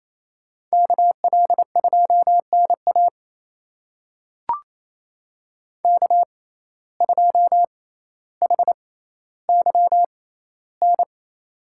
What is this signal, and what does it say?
Morse code, heard as a keyed tone, about 28 wpm, spaced as for 14 wpm, 700 Hz. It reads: KL2NA K2HYN
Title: USA Call Signs - 5 Characters - 6x Character Spacing - 28wpm